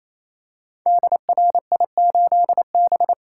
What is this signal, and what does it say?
Morse code, heard as a keyed tone, about 27 wpm, 700 Hz. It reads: DRI8B
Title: Code Groups - 5 Letter:Numbers - 28wpm